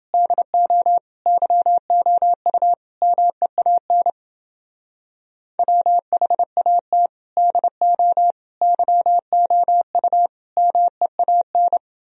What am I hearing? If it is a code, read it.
Morse code, heard as a keyed tone, about 28 wpm, 700 Hz. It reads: DO YOU MEAN WHAT DO YOU MEAN